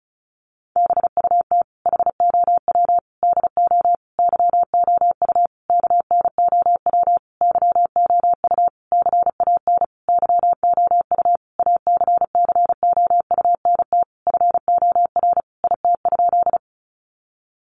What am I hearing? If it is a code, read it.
Morse code, heard as a keyed tone, about 34 wpm, 700 Hz. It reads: BUT HOW DO YOU KNOW YOU CAN YOU ACCOUNT FOR IT?
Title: Sentences from Top 900 Words 35wpm